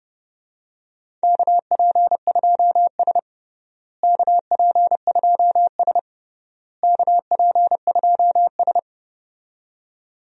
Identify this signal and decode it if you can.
Morse code, heard as a keyed tone, about 30 wpm, 700 Hz. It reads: KP2S KP2S KP2S
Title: DXCC-100 Call Signs - 4 Characters - Repeated 3x - 30wpm